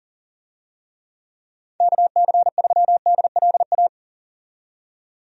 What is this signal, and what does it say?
Morse code, heard as a keyed tone, about 38 wpm, 700 Hz. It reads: KC3DLA
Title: USA Call Signs - 6 Characters - 40wpm